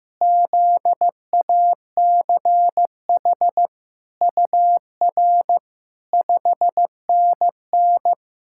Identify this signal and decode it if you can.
Morse code, heard as a keyed tone, about 16 wpm, 700 Hz. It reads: ZACH UR 5NN
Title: SOTA QSO Practice 15wpm